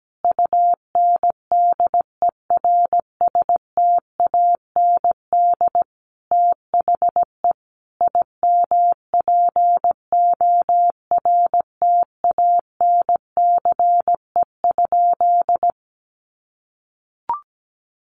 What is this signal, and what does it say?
Morse code, heard as a keyed tone, about 17 wpm, 700 Hz. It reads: UNDERSTAND THE IMPORTANCE?
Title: Sentences from Top 1200 Words 17wpm